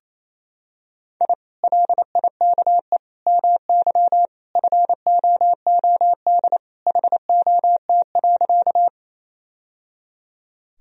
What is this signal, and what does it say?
Morse code, heard as a keyed tone, about 27 wpm, 700 Hz. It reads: I LIKE MY FOOD HOT.
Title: Sentences from Top 1000 Words - Review Part 3 - 28wpm